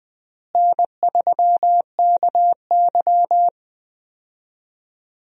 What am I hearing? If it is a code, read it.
Morse code, heard as a keyed tone, about 20 wpm, 700 Hz. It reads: N3KY
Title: USA Call Signs - 4 Characters - Repeated 2x - 20wpm